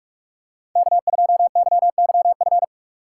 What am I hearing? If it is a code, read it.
Morse code, heard as a keyed tone, about 47 wpm, 700 Hz. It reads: K2YYF
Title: USA Call Signs - 5 Characters - 3x Character Spacing - 45wpm